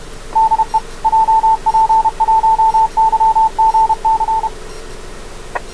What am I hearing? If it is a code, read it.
Morse code, heard as a keyed tone, about 31 wpm, 890 Hz, with radio noise. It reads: DE JP1YGC